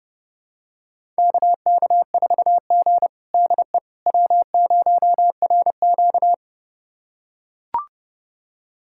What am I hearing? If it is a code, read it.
Morse code, heard as a keyed tone, about 30 wpm, 700 Hz. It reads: KK4G DE W0RQ
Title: USA Callsign DE Callsign - 4 Characters - 30wpm